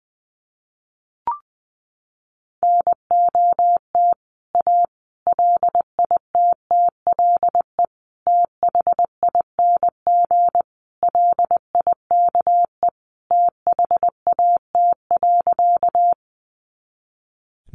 Morse code, heard as a keyed tone, about 21 wpm, 700 Hz. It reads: NOT A LITTLE THING LIKE THAT.
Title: A Sound Of Thunder 20wpm